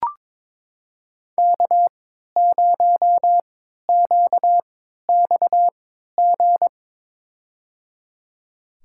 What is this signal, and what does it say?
Morse code, heard as a keyed tone, about 22 wpm, spaced as for 12 wpm, 700 Hz. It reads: K0QXG
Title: USA Call Signs - 5 Characters - 3x Character Spacing - 22wpm